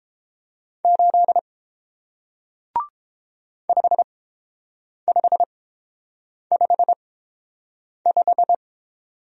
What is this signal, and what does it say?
Morse code, heard as a keyed tone, about 31 wpm, spaced as for 7 wpm, 700 Hz. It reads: 8 5555